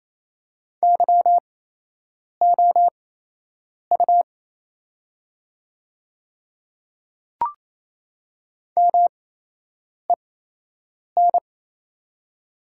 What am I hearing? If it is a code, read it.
Morse code, heard as a keyed tone, about 27 wpm, spaced as for 8 wpm, 690 Hz. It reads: YOU MEN
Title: 8x Character Spacing - Top 200 Words - 28wpm